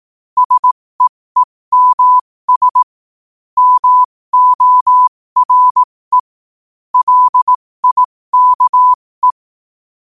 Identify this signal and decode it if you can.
Morse code, heard as a keyed tone, about 18 wpm, spaced as for 15 wpm, 1000 Hz. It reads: SEEMS MORE LIKE
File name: qotdfiles / 15wpm